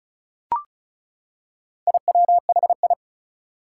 Morse code, heard as a keyed tone, about 36 wpm, 700 Hz. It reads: IWHI
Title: Four Challenging Letter-Number Combinations 35wpm